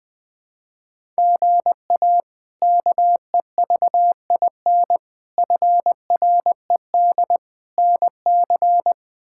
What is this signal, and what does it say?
Morse code, heard as a keyed tone, about 21 wpm, 700 Hz. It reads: GA KEVIN FRED NC
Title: K1USN SST QSO Practice - 20wpm